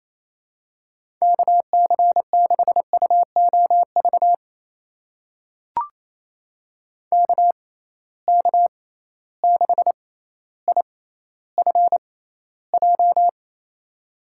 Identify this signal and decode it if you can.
Morse code, heard as a keyed tone, about 28 wpm, spaced as for 14 wpm, 700 Hz. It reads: KC6UOV KK6IFJ
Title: USA Call Signs - 6 Characters - 6x Character Spacing - 28wpm